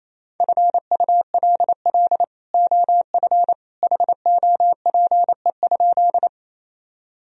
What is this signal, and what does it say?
Morse code, heard as a keyed tone, about 28 wpm, 700 Hz. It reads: FULL OF HOPE?
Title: Sentences from Top 1000 Words - Review Part 3 - 28wpm